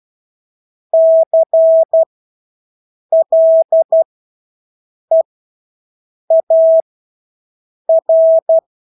Morse code, heard as a keyed tone, about 12 wpm, spaced as for 6 wpm, 650 Hz. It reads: CLEAR
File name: Engelske ord 12-6 wpm